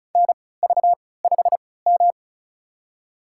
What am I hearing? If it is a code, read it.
Morse code, heard as a keyed tone, about 36 wpm, spaced as for 20 wpm, 700 Hz. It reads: NV5M
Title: USA Call Signs - 4 Characters - 3x Character Spacing - 35wpm